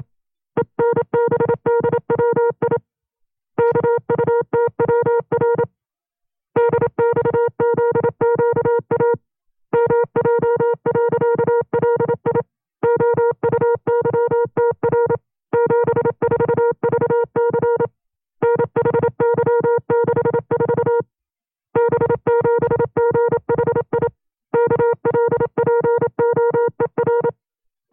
Morse code, heard as a keyed tone, about 27 wpm, 470 Hz. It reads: ENBDWI KUTWR DXZQA MJ.LI OUYTR 74VC NHY64 B7GHI KLPOER
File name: 25WPM